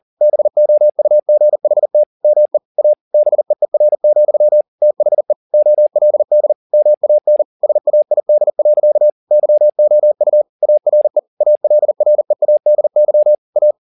Morse code, heard as a keyed tone, about 40 wpm, 600 Hz. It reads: BOUGHT ME A BEER, THE OLD MAN SAID. YOU ARE ALREADY A